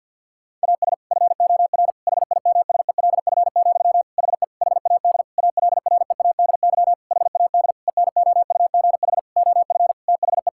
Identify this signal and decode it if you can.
Morse code, heard as a keyed tone, about 50 wpm, 700 Hz. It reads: AS FOR HIMSELF, HE HAD ALREADY HAD ENOUGH OF THE
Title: Sentences from Top 1600 Words 50wpm